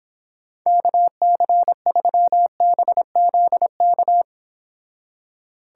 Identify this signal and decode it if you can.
Morse code, heard as a keyed tone, about 31 wpm, 700 Hz. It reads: KC3BZK